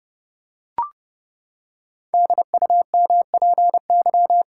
Morse code, heard as a keyed tone, about 30 wpm, 700 Hz. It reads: DUMPY
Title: Pangram Challenge v2 - 30wpm